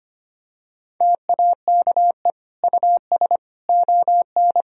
Morse code, heard as a keyed tone, about 25 wpm, 700 Hz. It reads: TAKE US ON
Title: Sentences from Top 1000 Words - Review Part 4 - 25wpm